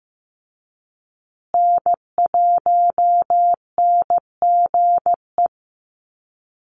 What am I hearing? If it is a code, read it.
Morse code, heard as a keyed tone, about 15 wpm, 700 Hz. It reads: N1NGE
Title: USA Call Signs Repeated 3x 15wpm